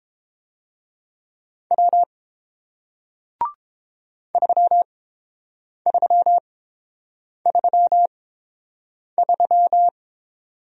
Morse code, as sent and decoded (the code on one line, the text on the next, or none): .-- / ...-- ...-- ...-- ...--
W 3333